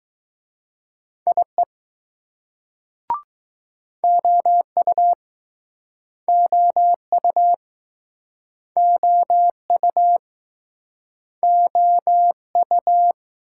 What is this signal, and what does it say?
Morse code, heard as a keyed tone, about 20 wpm, 700 Hz. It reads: IE OU OU OU OU